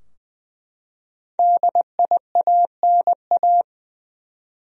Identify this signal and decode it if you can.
Morse code, heard as a keyed tone, about 20 wpm, 700 Hz. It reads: DIANA
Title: USA Common Names - 20wpm